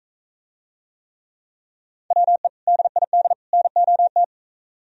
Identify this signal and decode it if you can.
Morse code, heard as a keyed tone, about 44 wpm, 700 Hz. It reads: WE DID NOT